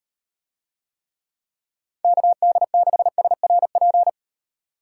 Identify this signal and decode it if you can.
Morse code, heard as a keyed tone, about 38 wpm, 700 Hz. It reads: KD6SRP